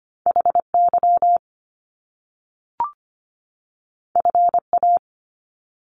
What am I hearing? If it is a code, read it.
Morse code, heard as a keyed tone, about 25 wpm, 700 Hz. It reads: HY FA
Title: Top Two Letter Combinations in Words 25wpm